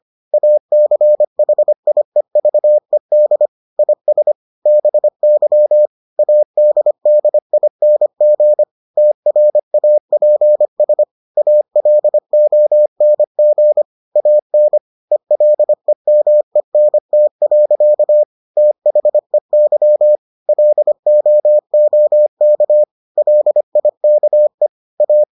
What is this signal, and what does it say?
Morse code, heard as a keyed tone, about 25 wpm, 600 Hz. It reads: ACHIEVED IS BY ADDING TRAPS ALONG AN ELEMENT. THEY LOOK LIKE A